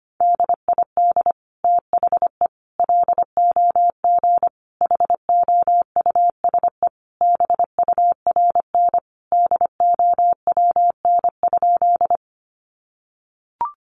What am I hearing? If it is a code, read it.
Morse code, heard as a keyed tone, about 25 wpm, 700 Hz. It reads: DID THE LOG HOUSE BURN DOWN?
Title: Sentences from Top 1400 Words 25wpm